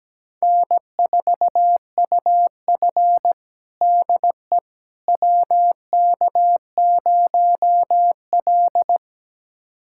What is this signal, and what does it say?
Morse code, heard as a keyed tone, about 17 wpm, 700 Hz. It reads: N4UF DE WK0L
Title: USA Callsign DE Callsign - 4 Characters - 17wpm